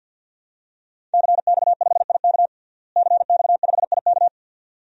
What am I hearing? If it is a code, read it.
Morse code, heard as a keyed tone, about 52 wpm, 700 Hz. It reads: CX5IK CX5IK
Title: DXCC-100 Call Signs - 5 Characters - Repeated 2x - 50wpm